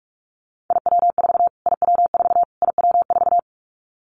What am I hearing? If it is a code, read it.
Morse code, heard as a keyed tone, about 44 wpm, 700 Hz. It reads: IW4 IW4 IW4
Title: Callsign Prefixes for Top 25 DXCC - Repeated 3x - 45wpm